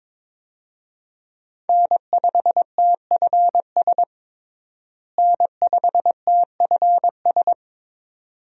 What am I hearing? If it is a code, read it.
Morse code, heard as a keyed tone, about 21 wpm, 700 Hz. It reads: N5TFS N5TFS
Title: USA Call Signs - 5 Characters - Repeated 2x - 22wpm